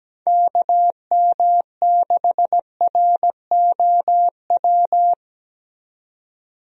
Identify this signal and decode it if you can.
Morse code, heard as a keyed tone, about 17 wpm, 700 Hz. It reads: KM6ROW
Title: USA Call Signs - 6 Characters - 17wpm